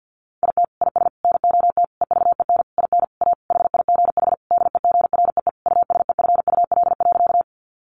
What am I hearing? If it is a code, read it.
Morse code, heard as a keyed tone, about 50 wpm, 700 Hz. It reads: IT IS NOT EVEN IN A HIGH DEGREE USEFUL.
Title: Sentences from Top 1500 Words 50wpm